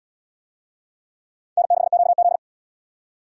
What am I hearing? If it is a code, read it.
Morse code, heard as a keyed tone, about 70 wpm, 700 Hz. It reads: N6QZ